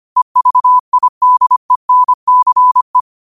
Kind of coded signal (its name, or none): Morse code